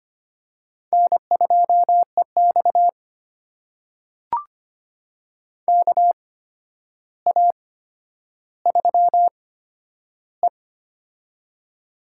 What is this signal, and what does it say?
Morse code, heard as a keyed tone, about 25 wpm, spaced as for 11 wpm, 700 Hz. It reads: N2EX KA3E
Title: USA Call Signs - 4 Characters - 8x Character Spacing - 25wpm